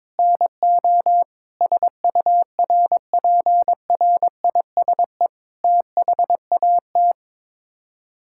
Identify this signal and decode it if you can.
Morse code, heard as a keyed tone, about 22 wpm, 700 Hz. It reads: NO SURPRISE THAT
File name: practice / 22wpm / Sets of 8 Words - 22wpm